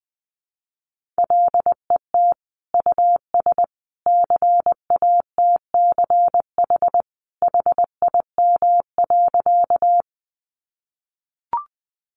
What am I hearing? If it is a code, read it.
Morse code, heard as a keyed tone, about 20 wpm, 700 Hz. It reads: LET US CATCH HIM.
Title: Sentences from Top 1500 Words 20wpm